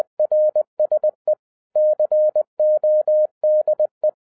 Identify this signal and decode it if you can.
Morse code, heard as a keyed tone, about 20 wpm, 610 Hz. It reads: RSE CODE
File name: 2011-07-mp3 / n08 A LOT HAS CHANGED 20wpm